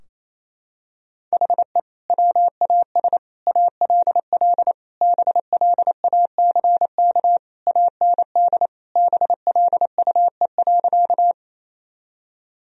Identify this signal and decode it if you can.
Morse code, heard as a keyed tone, about 28 wpm, 700 Hz. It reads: HE WAS ALL BLACK AND BLUE.
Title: Sentences from Top 1000 Words - Review Part 3 - 28wpm